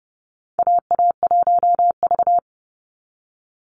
Morse code, heard as a keyed tone, about 30 wpm, 700 Hz. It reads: AA1V
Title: Call Signs Repeated 3x - 30wpm